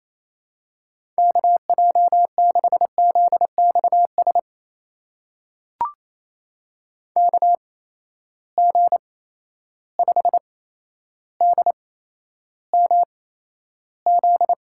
Morse code, heard as a keyed tone, about 28 wpm, spaced as for 12 wpm, 700 Hz. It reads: KJ6ZXS KG5DMZ
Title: USA Call Signs - 6 Characters - 8x Character Spacing - 28wpm